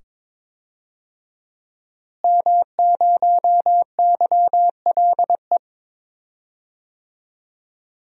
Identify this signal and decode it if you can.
Morse code, heard as a keyed tone, about 22 wpm, 700 Hz. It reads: M0YLE
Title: DXCC-100 Call Signs - 5 Characters - Repeated 2x - 22wpm